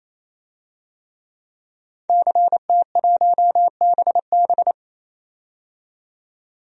Morse code, heard as a keyed tone, about 27 wpm, 690 Hz. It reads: CT1BB
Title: DXCC-100 Call Signs - 5 Characters - 28wpm